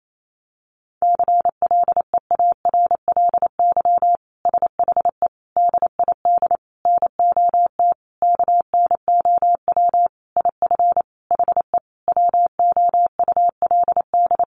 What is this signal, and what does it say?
Morse code, heard as a keyed tone, about 28 wpm, 700 Hz. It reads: CLEARLY SHE DID NOT KNOW IF HE WOULD
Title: Sentences from Top 1300 Words 28wpm